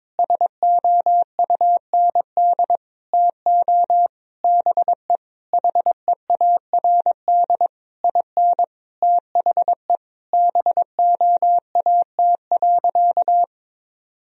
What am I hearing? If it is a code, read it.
Morse code, heard as a keyed tone, about 22 wpm, 700 Hz. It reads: SOUND TO BE HEARD IN THE BOAT.
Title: Sentences from Top 1000 Words - Review Part 3 - 22wpm